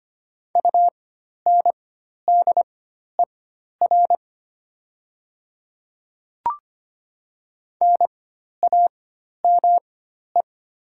Morse code, heard as a keyed tone, about 25 wpm, spaced as for 12 wpm, 700 Hz. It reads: UNDER NAME
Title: 4x Character Spacing - Top 200 Words - 25wpm